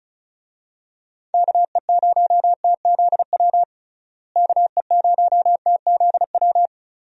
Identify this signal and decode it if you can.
Morse code, heard as a keyed tone, about 34 wpm, 700 Hz. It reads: KE0TZW KE0TZW
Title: USA Call Signs - 6 Characters - Repeated 3x - 35wpm